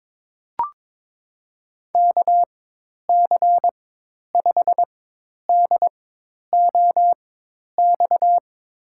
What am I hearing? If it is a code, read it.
Morse code, heard as a keyed tone, about 22 wpm, spaced as for 10 wpm, 700 Hz. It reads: KC5DOX
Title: USA Call Signs - 6 Characters - 4x Character Spacing - 22wpm